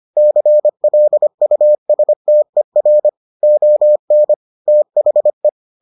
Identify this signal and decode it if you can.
Morse code, heard as a keyed tone, about 25 wpm, 600 Hz. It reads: CLUSTER ON THE